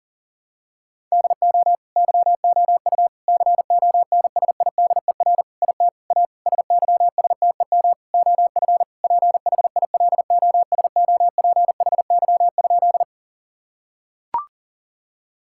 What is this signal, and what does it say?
Morse code, heard as a keyed tone, about 40 wpm, 700 Hz. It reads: DO YOU CONSIDER IT A SYSTEM OF PHILOSOPHY?
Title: Sentences from Top 1000 Words - Review Part 6 - 40wpm